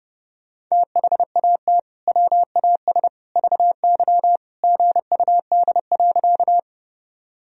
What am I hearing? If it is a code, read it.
Morse code, heard as a keyed tone, about 30 wpm, 700 Hz. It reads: THAT WAS VY GUD.
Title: Sentences from Top 200 Words and Abbreviations 30wpm